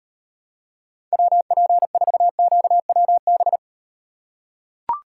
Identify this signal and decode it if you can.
Morse code, heard as a keyed tone, about 38 wpm, 700 Hz. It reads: WP4QWB